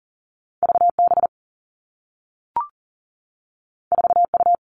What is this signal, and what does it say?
Morse code, heard as a keyed tone, about 40 wpm, 700 Hz. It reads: VB 4U
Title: Two Challenging Letter-Number Combinations 40wpm